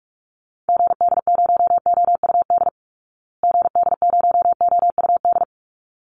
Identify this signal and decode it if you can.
Morse code, heard as a keyed tone, about 44 wpm, 700 Hz. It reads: GD0OUD GD0OUD
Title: Call Signs Repeated 3x - 45wpm